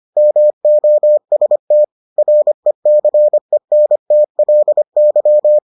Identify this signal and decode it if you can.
Morse code, heard as a keyed tone, about 25 wpm, 600 Hz. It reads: MOST RECENTLY